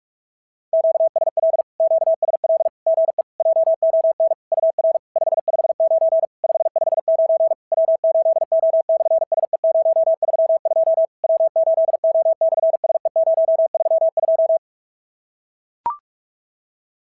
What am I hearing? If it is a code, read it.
Morse code, heard as a keyed tone, about 44 wpm, 660 Hz. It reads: QSL QSL GE JON UR 559 559 W8O/SE032 W8O/SE032
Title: SOTA QSO Practice 45wpm